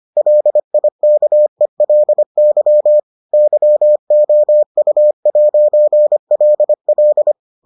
Morse code, heard as a keyed tone, about 25 wpm, 600 Hz. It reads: LIKELY YOU'LL